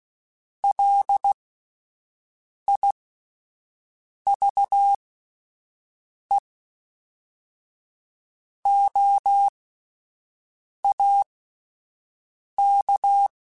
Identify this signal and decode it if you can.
Morse code, heard as a keyed tone, about 16 wpm, spaced as for 5 wpm, 800 Hz. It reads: LIVEOAK